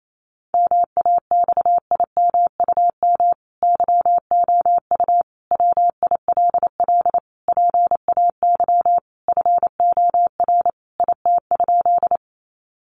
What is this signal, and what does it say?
Morse code, heard as a keyed tone, about 28 wpm, 700 Hz. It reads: MAXIMUM YOU WILL PAY FOR IT?
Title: Sentences from Top 1300 Words 28wpm